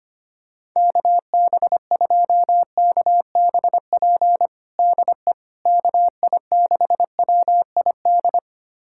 Morse code, heard as a keyed tone, about 25 wpm, 700 Hz. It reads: KB2KBP DE KI6WID
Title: USA Callsign DE Callsign - 6 Characters - 25wpm